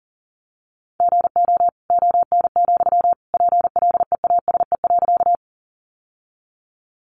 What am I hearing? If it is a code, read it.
Morse code, heard as a keyed tone, about 40 wpm, 700 Hz. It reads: GO ON, PLEASE.
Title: Sentences from Top 1500 Words 40wpm